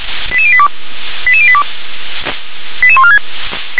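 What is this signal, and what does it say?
Five-tone selective call (ZVEI), heard as quick runs of five tones, tones of about 70 ms. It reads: 88082 88082 88146
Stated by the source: ZVEI five-tone selective call 88714 received by SDR.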